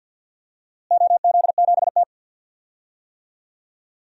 Morse code, heard as a keyed tone, about 53 wpm, 700 Hz. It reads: OZ7T
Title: DXCC-100 Call Signs - 4 Characters - Repeated 2x - 50wpm